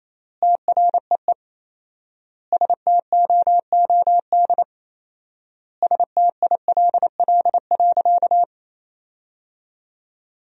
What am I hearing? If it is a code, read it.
Morse code, heard as a keyed tone, about 28 wpm, 700 Hz. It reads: TREE STOOD STILL.
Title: 4x Word-Spacing - Sentences from Top 500 Words - 28wpm